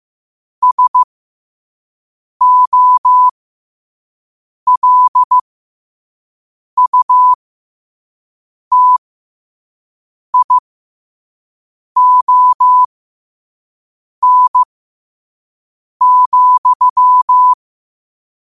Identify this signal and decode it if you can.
Morse code, heard as a keyed tone, about 15 wpm, spaced as for 5 wpm, 1000 Hz. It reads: SOLUTION,
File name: qotdfiles / 5wpm